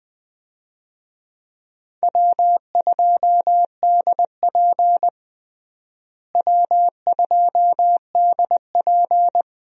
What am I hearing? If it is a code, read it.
Morse code, heard as a keyed tone, about 20 wpm, 700 Hz. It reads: W2DP W2DP
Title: USA Call Signs - 4 Characters - Repeated 3x - 20wpm